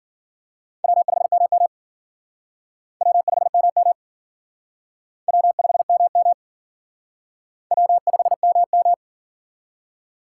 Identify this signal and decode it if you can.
Morse code, heard as a keyed tone, about 50 wpm, 700 Hz. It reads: W5MM W5MM W5MM W5MM